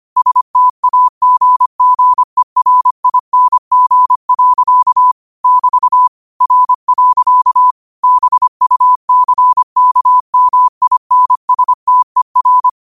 Morse code, heard as a keyed tone, about 25 wpm, 1000 Hz. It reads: STAGGERING. <BT> R. BUCKMINSTER
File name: qotdfiles / 25wpm